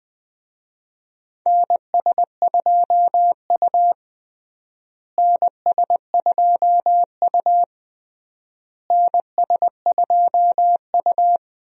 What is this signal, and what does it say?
Morse code, heard as a keyed tone, about 21 wpm, 700 Hz. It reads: NS2U NS2U NS2U
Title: USA Call Signs - 4 Characters - Repeated 3x - 20wpm